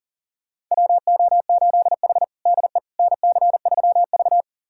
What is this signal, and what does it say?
Morse code, heard as a keyed tone, about 40 wpm, 700 Hz. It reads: WO8H DE NC3V
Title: USA Callsign DE Callsign - 4 Characters - 40wpm